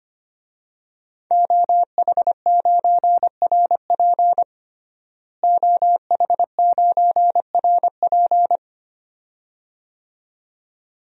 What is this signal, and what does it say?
Morse code, heard as a keyed tone, about 25 wpm, 700 Hz. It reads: OH9RP OH9RP
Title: DXCC-100 Call Signs - 5 Characters - Repeated 2x - 25wpm